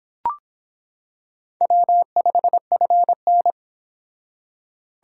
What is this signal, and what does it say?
Morse code, heard as a keyed tone, about 25 wpm, 700 Hz. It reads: W5FN